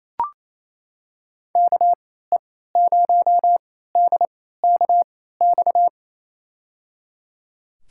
Morse code, heard as a keyed tone, about 28 wpm, spaced as for 16 wpm, 700 Hz. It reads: KE0DKX
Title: USA Call Signs - 6 Characters - 3x Character Spacing - 28wpm